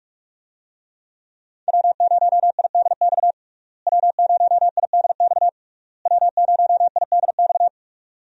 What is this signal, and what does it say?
Morse code, heard as a keyed tone, about 46 wpm, 700 Hz. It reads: W0IDX W0IDX W0IDX
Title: USA Call Signs - 5 Characters - Repeated 3x - 45wpm